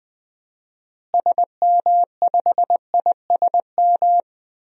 Morse code, heard as a keyed tone, about 20 wpm, 700 Hz. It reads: SM5ISM
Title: DXCC-100 Call Signs - 6 Characters - Repeated 3x - 20wpm